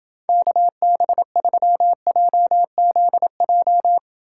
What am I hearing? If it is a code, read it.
Morse code, heard as a keyed tone, about 28 wpm, 700 Hz. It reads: KB3JZJ